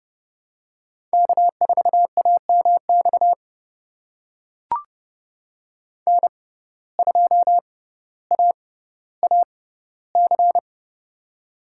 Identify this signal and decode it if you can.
Morse code, heard as a keyed tone, about 32 wpm, spaced as for 16 wpm, 700 Hz. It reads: K4AMX N2AAC
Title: USA Call Signs - 5 Characters - 6x Character Spacing - 30wpm